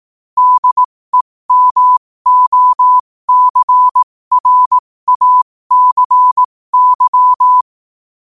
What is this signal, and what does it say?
Morse code, heard as a keyed tone, about 18 wpm, spaced as for 15 wpm, 1000 Hz. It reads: DEMOCRACY
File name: qotdfiles / 15wpm